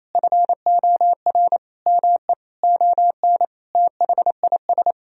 Morse code, heard as a keyed tone, about 28 wpm, 700 Hz. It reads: FOR ME ON THIS
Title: Sentences from Top 1000 Words - Review Part 2 - 28wpm